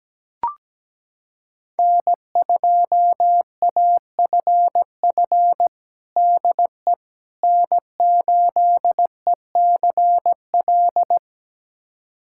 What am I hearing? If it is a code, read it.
Morse code, heard as a keyed tone, about 17 wpm, 700 Hz. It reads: N2AFF DE N8ECL
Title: USA Callsign DE Callsign - 5 Characters - 17wpm